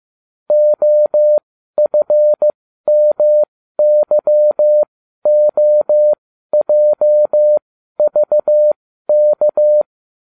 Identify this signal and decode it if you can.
Morse code, heard as a keyed tone, about 15 wpm, spaced as for 12 wpm, 600 Hz. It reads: OFMYOJVK